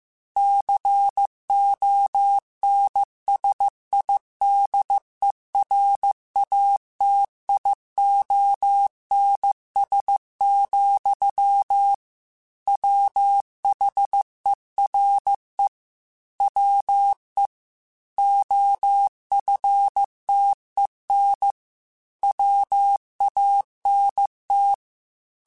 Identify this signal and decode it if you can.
Morse code, heard as a keyed tone, about 15 wpm, 800 Hz. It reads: CONSIDERATIONS, WHERE WE OFTEN WANT
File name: Technical Subjects / Technical Subjects-mp3s / Sw Pwr Sup @13